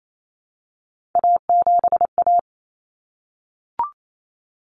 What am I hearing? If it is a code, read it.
Morse code, heard as a keyed tone, about 28 wpm, 700 Hz. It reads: A7A
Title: Common Cut Numbers - 3-Digit 28wpm